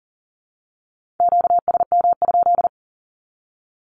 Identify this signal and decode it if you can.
Morse code, heard as a keyed tone, about 40 wpm, 700 Hz. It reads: QSM?
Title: Q-Codes 40wpm